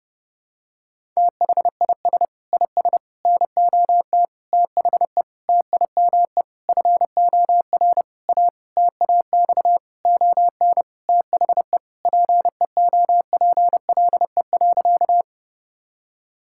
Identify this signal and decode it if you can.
Morse code, heard as a keyed tone, about 30 wpm, 700 Hz. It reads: THIS IS NOT THE TIME FOR A TAX ON THE PEOPLE.
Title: Sentences from Top 1000 Words - Review Part 5 - 30wpm